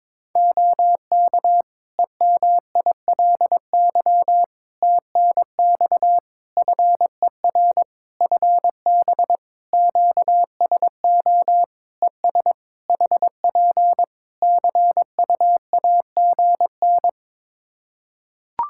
Morse code, heard as a keyed tone, about 22 wpm, 700 Hz. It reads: OK EMILY TNX FER FB QSO ES HP CUAGN
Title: Practice QSO - Exchange 4 - Ending - 22wpm